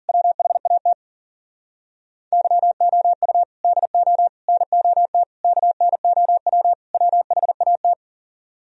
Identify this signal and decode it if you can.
Morse code, heard as a keyed tone, about 52 wpm, 700 Hz. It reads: WHAT YOU DO NOT KNOW WHAT